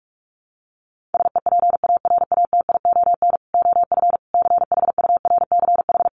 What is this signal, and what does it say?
Morse code, heard as a keyed tone, about 44 wpm, 700 Hz. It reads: SEPARATION OF CHURCH
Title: Sets of 5 Words 45wpm